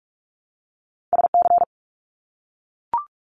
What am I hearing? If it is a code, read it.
Morse code, heard as a keyed tone, about 44 wpm, 700 Hz. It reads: SC
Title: US States Repeated 3x 45wpm